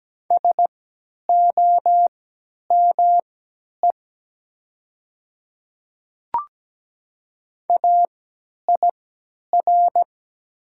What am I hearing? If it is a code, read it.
Morse code, heard as a keyed tone, about 17 wpm, spaced as for 10 wpm, 700 Hz. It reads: SOME AIR
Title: 3x Character Spacing - Top 200 Words - 17wpm